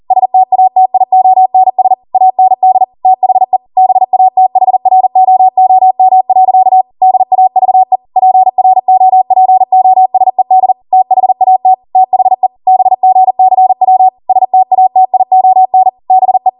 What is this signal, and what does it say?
Morse code, heard as a keyed tone, about 40 wpm, 750 Hz. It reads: STATIONS AND THE BATHROOM. DAVE PROPOSED THAT THE BGCW STATION BE